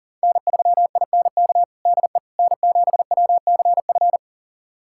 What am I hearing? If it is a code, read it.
Morse code, heard as a keyed tone, about 37 wpm, 700 Hz. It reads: N3INK DE N7WCF
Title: USA Callsign DE Callsign - 5 Characters - 40wpm